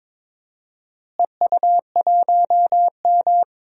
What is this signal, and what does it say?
Morse code, heard as a keyed tone, about 22 wpm, 700 Hz. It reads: EU1M
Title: DXCC-100 Call Signs - 4 Characters - Repeated 2x - 22wpm